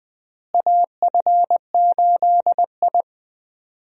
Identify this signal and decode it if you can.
Morse code, heard as a keyed tone, about 20 wpm, 700 Hz. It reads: AF8I
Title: USA Call Signs - 4 Characters - Repeated 3x - 20wpm